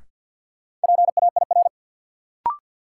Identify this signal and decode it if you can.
Morse code, heard as a keyed tone, about 47 wpm, 700 Hz. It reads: PAIR
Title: Sentences from Top 1200 Words - New Word Drill - 50wpm